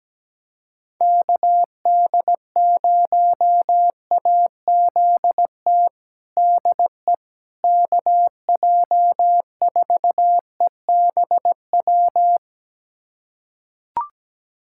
Morse code, heard as a keyed tone, about 17 wpm, 700 Hz. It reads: KD0AZT DE KJ4EBW
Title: USA Callsign DE Callsign 17wpm